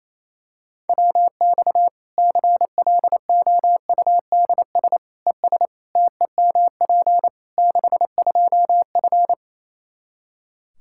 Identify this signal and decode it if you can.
Morse code, heard as a keyed tone, about 28 wpm, 700 Hz. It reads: WX CLOUDS ES TEMP 62F
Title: K1USN SST QSO Practice 28wpm